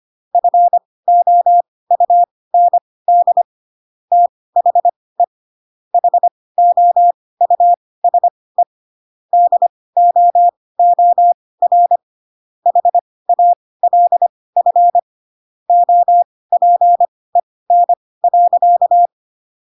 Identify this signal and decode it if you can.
Morse code, heard as a keyed tone, about 25 wpm, spaced as for 18 wpm, 700 Hz. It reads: FOUND THE HOUSE DOOR HALF OPEN.
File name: Pinocchio / Pinocchio ch4 12 25 ik1hge